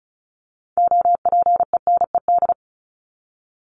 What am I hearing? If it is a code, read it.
Morse code, heard as a keyed tone, about 34 wpm, 700 Hz. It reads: OPENED
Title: ED Suffix - Top 100 Words with Base Words 35wpm